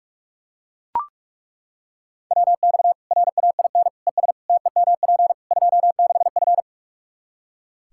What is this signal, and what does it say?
Morse code, heard as a keyed tone, about 42 wpm, 700 Hz. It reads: WX RAIN ES TEMP 26F
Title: Practice QSO - Exchange 2 - Info - 45wpm